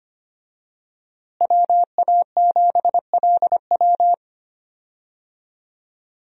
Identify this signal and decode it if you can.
Morse code, heard as a keyed tone, about 25 wpm, 700 Hz. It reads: WA7LW
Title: USA Call Signs - 5 Characters - Repeated 3x - 25wpm